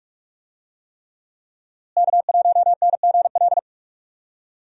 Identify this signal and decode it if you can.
Morse code, heard as a keyed tone, about 46 wpm, 700 Hz. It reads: K1NGL
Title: USA Call Signs - 5 Characters - Repeated 3x - 45wpm